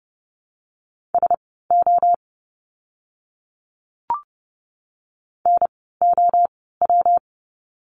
Morse code, heard as a keyed tone, about 30 wpm, spaced as for 17 wpm, 700 Hz. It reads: SO NOW
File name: course / 30:10wpm / 230 Lesson 37f - Letter x - Words 30wpm